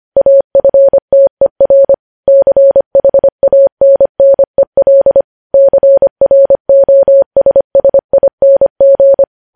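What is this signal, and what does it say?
Morse code, heard as a keyed tone, about 25 wpm, 550 Hz. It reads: AFTER CHANNEL CROSSING
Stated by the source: News headlines sent in Morse at 25 wpm.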